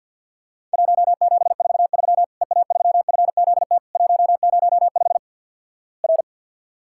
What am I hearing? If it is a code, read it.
Morse code, heard as a keyed tone, about 50 wpm, 700 Hz. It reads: 1743 EA3FZT 105 R
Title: Worked All Europe DX Contest - QTC Exchange with Repeats - 50wpm